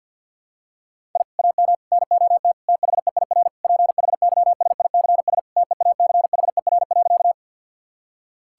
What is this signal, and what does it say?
Morse code, heard as a keyed tone, about 50 wpm, 700 Hz. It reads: I AM NOT THEIR PHYSICS TEACHER.
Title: Sentences from Top 1000 Words - Review Part 6 - 50wpm